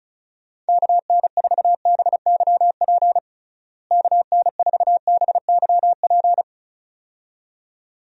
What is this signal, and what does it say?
Morse code, heard as a keyed tone, about 34 wpm, 700 Hz. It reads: KN4BYP KN4BYP
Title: USA Call Signs - 6 Characters - Repeated 2x - 35wpm